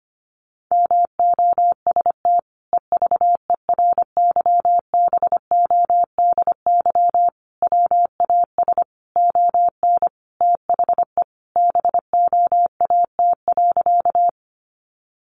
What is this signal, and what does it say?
Morse code, heard as a keyed tone, about 25 wpm, 700 Hz. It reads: MOST EVERYBODY WAS ON THE BOAT.
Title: Sentences from Top 1400 Words 25wpm